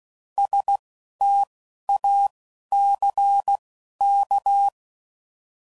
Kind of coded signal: Morse code